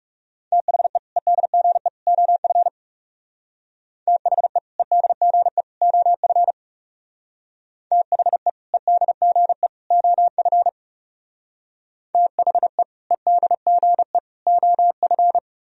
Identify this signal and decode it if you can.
Morse code, heard as a keyed tone, about 36 wpm, 700 Hz. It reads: THE EDGE OF THE EDGE OF THE EDGE OF THE EDGE OF
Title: Speed-Racing - Sets of 3 Words 30wpm